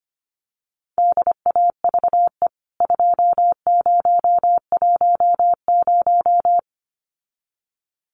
Morse code, heard as a keyed tone, about 25 wpm, 690 Hz. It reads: DAVE 2010
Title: CWT Exchange 25wpm